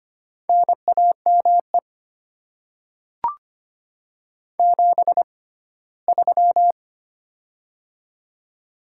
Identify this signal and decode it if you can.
Morse code, heard as a keyed tone, about 25 wpm, spaced as for 17 wpm, 700 Hz. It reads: NAME 73
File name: Top 20 QSO Elements - 6x Character Spacing - 25wpm